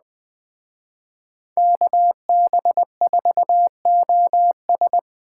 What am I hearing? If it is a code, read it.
Morse code, heard as a keyed tone, about 20 wpm, 700 Hz. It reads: KB4OS